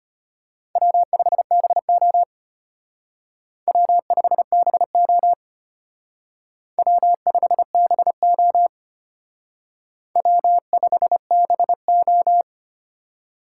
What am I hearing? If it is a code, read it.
Morse code, heard as a keyed tone, about 31 wpm, 700 Hz. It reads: W5BO W5BO W5BO W5BO